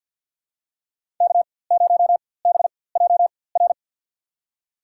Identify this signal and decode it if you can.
Morse code, heard as a keyed tone, about 53 wpm, spaced as for 24 wpm, 700 Hz. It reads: K0BJR
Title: USA Call Signs - 5 Characters - 4x Character Spacing - 50wpm